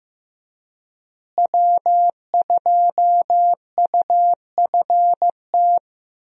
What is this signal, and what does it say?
Morse code, heard as a keyed tone, about 15 wpm, 700 Hz. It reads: W2UFT
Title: USA Call Signs - 5 Characters - Repeated 3x - 15wpm